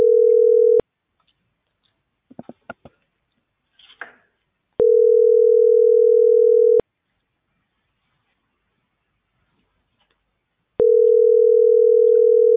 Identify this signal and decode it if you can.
Ringback tone (North America), 440 + 480 Hz.